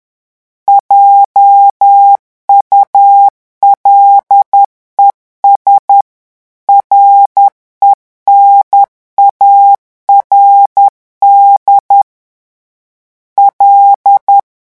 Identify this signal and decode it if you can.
Morse code, heard as a keyed tone, about 10 wpm, 800 Hz. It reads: JULES RENARD L
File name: morse mp3 / revision francais 10 mots minute